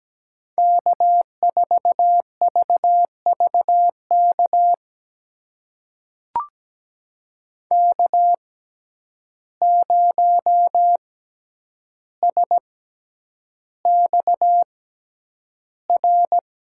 Morse code, heard as a keyed tone, about 17 wpm, spaced as for 9 wpm, 700 Hz. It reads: K4VVK K0SXR
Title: USA Call Signs - 5 Characters - 6x Character Spacing - 17wpm